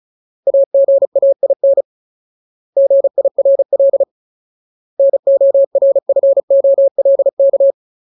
Morse code, heard as a keyed tone, about 34 wpm, 550 Hz. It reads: AGAIN GIRL NORFOLK
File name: Ultimate Word List / 35WPM Ultimate